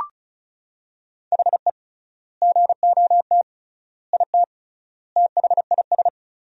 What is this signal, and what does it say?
Morse code, heard as a keyed tone, about 34 wpm, 700 Hz. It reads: HE GOT IT THIS
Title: 3x Word-Spacing - Sentences from Top 400 Words - 35wpm